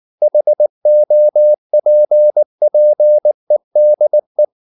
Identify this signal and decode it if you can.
Morse code, heard as a keyed tone, about 19 wpm, 600 Hz. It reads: HOPPEDE
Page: Ferie 19 wpm